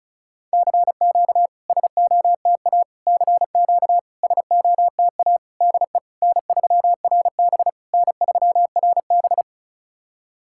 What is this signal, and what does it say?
Morse code, heard as a keyed tone, about 36 wpm, 700 Hz. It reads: CQ SOTA CQ SOTA DE N3RB N3RB
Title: SOTA QSO Practice 35wpm